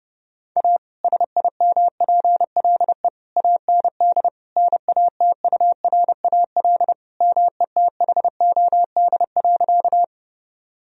Morse code, heard as a keyed tone, about 32 wpm, 700 Hz. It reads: A SIMPLE AND NATURAL METHOD.
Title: Sentences from Top 1000 Words - Review Part 4 - 30wpm